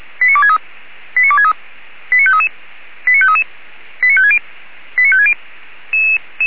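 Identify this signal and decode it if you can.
Five-tone selective call (ZVEI), heard as quick runs of five tones, tones of about 70 ms. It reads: 89263 89263 89533 89533 89560 89560
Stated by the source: Five-tone ZVEI selective call received by SDR.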